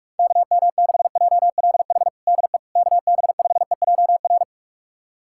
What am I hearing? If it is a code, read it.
Morse code, heard as a keyed tone, about 46 wpm, 700 Hz. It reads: KM6JLH DE KB5EJR
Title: USA Callsign DE Callsign - 6 Characters - 45wpm